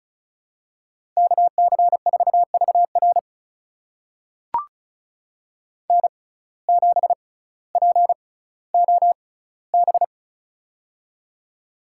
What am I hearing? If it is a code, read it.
Morse code, heard as a keyed tone, about 36 wpm, spaced as for 18 wpm, 700 Hz. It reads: KC4VR N7POB
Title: USA Call Signs - 6x Character Spacing - 35wpm